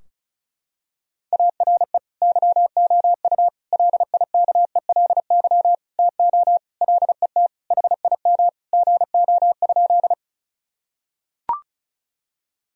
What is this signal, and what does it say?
Morse code, heard as a keyed tone, about 36 wpm, 700 Hz. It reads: ARE YOU LIKELY TO LET HIM GO?
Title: Sentences from Top 1000 Words - Review Part 4 - 35wpm